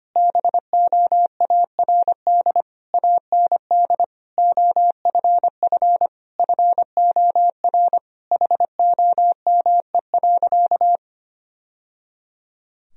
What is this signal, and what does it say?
Morse code, heard as a keyed tone, about 25 wpm, 700 Hz. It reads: BOARD AND OFF FOR HOME.
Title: Sentences from Top 1000 Words - Review Part 4 - 25wpm